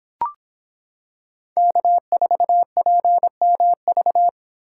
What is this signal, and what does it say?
Morse code, heard as a keyed tone, about 26 wpm, 700 Hz. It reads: K4PMV